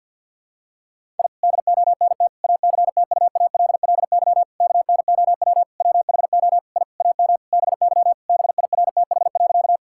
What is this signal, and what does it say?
Morse code, heard as a keyed tone, about 50 wpm, 700 Hz. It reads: I DONT ACTUALLY KNOW WHO I AM BY BIRTH.
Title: Sentences from Top 1000 Words - Review Part 5 - 50wpm